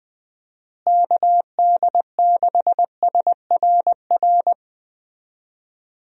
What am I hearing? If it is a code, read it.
Morse code, heard as a keyed tone, about 20 wpm, 700 Hz. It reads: KD6SRR
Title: USA Call Signs - 6 Characters - Repeated 3x - 20wpm